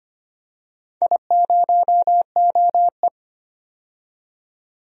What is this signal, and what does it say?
Morse code, heard as a keyed tone, about 25 wpm, 700 Hz. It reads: I0OE
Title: DXCC-100 Call Signs - 4 Characters - 25wpm